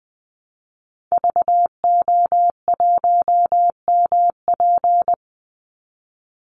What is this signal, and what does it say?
Morse code, heard as a keyed tone, about 20 wpm, 700 Hz. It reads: VO1MP
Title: Call Signs Repeated 3x - 20wpm